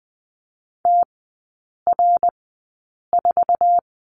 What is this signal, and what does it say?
Morse code, heard as a keyed tone, about 20 wpm, spaced as for 8 wpm, 690 Hz. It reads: TR4
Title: 060 Lesson 11e - Letter h - Three Characters 20wpm